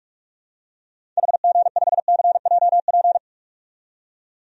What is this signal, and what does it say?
Morse code, heard as a keyed tone, about 42 wpm, 700 Hz. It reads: HG5CJP
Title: DXCC-100 Call Signs - 6 Characters - 45wpm